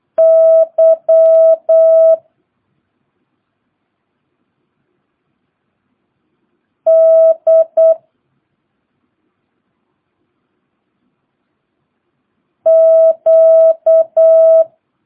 Morse code, heard as a keyed tone, about 8 wpm, 650 Hz. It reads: Y D Q
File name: jh8cbh / 2022cw